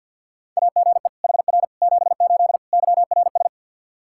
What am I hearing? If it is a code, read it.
Morse code, heard as a keyed tone, about 47 wpm, 700 Hz. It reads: AGE HR 78 YRS
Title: Practice QSO - Exchange 3 - More Info - 50wpm